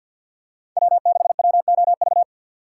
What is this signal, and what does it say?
Morse code, heard as a keyed tone, about 51 wpm, 700 Hz. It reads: W6WOV
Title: USA Call Signs - 5 Characters - 2x Character Spacing - 50wpm